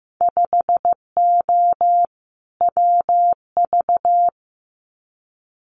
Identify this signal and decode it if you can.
Morse code, heard as a keyed tone, about 15 wpm, 700 Hz. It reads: 5O WV
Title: Winter Field Day Exchanges Encore 15wpm